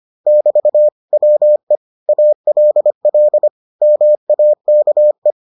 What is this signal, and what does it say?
Morse code, heard as a keyed tone, about 25 wpm, 600 Hz. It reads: <BT> WE ALL MAKE